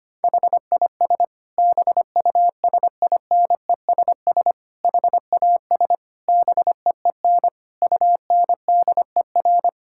Morse code, heard as a keyed tone, about 25 wpm, 700 Hz. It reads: HIS BUSINESS HAS BEEN UNDER
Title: Sentences from Top 1000 Words - Review Part 6 - 25wpm